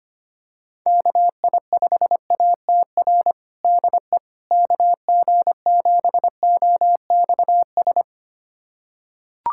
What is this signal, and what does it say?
Morse code, heard as a keyed tone, about 25 wpm, 700 Hz. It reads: KI5ATR DE KG7OXS
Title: USA Callsign DE Callsign 25wpm